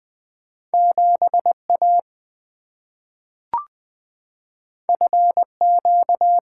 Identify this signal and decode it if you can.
Morse code, heard as a keyed tone, about 20 wpm, 700 Hz. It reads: 7 A F Q
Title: Two Letter-Number Combinations - 20wpm